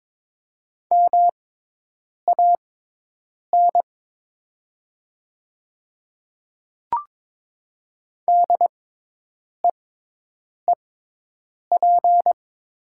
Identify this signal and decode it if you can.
Morse code, heard as a keyed tone, about 22 wpm, spaced as for 8 wpm, 700 Hz. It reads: MAN DEEP
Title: 6x Character Spacing - Top 500 Words - 22wpm